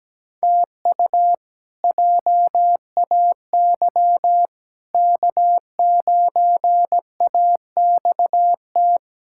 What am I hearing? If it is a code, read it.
Morse code, heard as a keyed tone, about 17 wpm, 700 Hz. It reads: TU JAY K9AXT
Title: K1USN SST QSO Practice - 17wpm